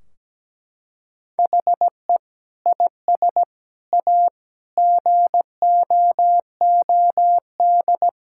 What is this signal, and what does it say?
Morse code, heard as a keyed tone, about 17 wpm, 700 Hz. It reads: HE IS A GOOD
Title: Sentences from Top 1600 Words 17wpm